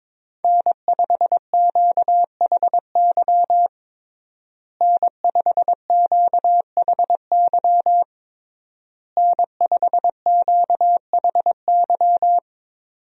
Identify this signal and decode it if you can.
Morse code, heard as a keyed tone, about 22 wpm, 700 Hz. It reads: N5QHY N5QHY N5QHY
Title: USA Call Signs - 5 Characters - Repeated 3x - 22wpm